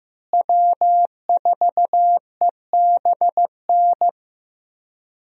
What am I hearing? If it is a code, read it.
Morse code, heard as a keyed tone, about 15 wpm, 700 Hz. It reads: W4EBN
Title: USA Call Signs - 5 Characters - Repeated 2x - 15wpm